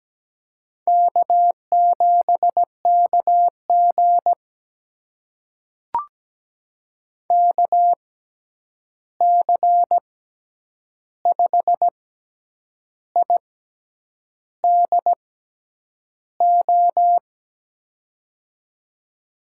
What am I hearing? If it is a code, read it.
Morse code, heard as a keyed tone, about 17 wpm, spaced as for 8 wpm, 700 Hz. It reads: K7KG KC5IDO
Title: USA Call Signs - 6x Character Spacing - 17wpm